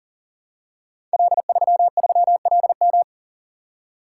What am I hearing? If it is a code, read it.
Morse code, heard as a keyed tone, about 41 wpm, 700 Hz. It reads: L33LM
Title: Code Groups - 5 Letter:Numbers - 40wpm